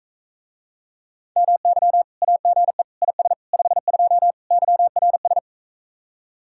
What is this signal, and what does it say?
Morse code, heard as a keyed tone, about 44 wpm, 700 Hz. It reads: MY AGE IS 52 YRS